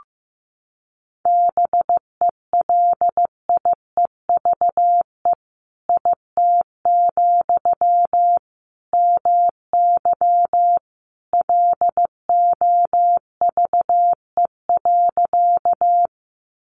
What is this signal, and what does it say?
Morse code, heard as a keyed tone, about 15 wpm, 700 Hz. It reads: BELIEVE IT, MY LOVE.
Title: Sentences from Top 1000 Words 15wpm